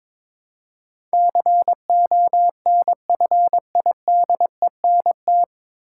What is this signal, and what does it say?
Morse code, heard as a keyed tone, about 23 wpm, 700 Hz. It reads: CONFIDENT
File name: 9 Letter Words - First 3 Letters Given - 22wpm